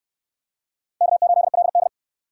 Morse code, heard as a keyed tone, about 59 wpm, 700 Hz. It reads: K8LD